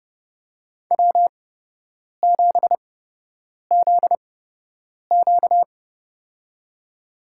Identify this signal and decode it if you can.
Morse code, heard as a keyed tone, about 30 wpm, spaced as for 8 wpm, 700 Hz. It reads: W7ZQ
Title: USA Call Signs - 8x Character Spacing - 30wpm